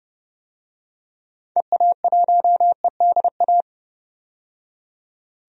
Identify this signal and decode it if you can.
Morse code, heard as a keyed tone, about 32 wpm, 700 Hz. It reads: EA1EDA
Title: DXCC-100 Call Signs - 6 Characters - 30wpm